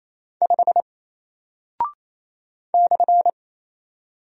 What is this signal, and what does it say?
Morse code, heard as a keyed tone, about 28 wpm, 700 Hz. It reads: 5 /
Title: Single Character (Letter:Number:Punctuation:Prosign) Alt 28wpm